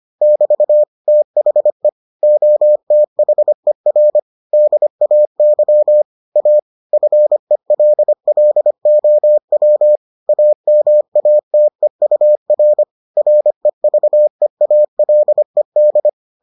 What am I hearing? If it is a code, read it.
Morse code, heard as a keyed tone, about 25 wpm, 600 Hz. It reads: <BT> THE OTHER DAY A FELLOW AMATEUR REVEALED